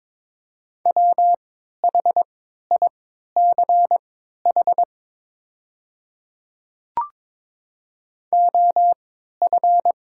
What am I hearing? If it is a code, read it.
Morse code, heard as a keyed tone, about 23 wpm, spaced as for 13 wpm, 700 Hz. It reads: WHICH OF
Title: 3x Character Spacing - Top 100 Words - 22wpm